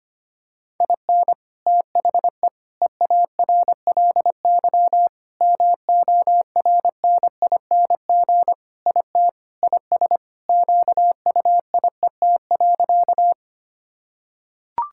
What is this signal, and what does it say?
Morse code, heard as a keyed tone, about 25 wpm, 700 Hz. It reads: IN THE EARLY MORNING IT IS QUIET.
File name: practice / 25wpm / Sentences with 3-Word Phrases - 25wpm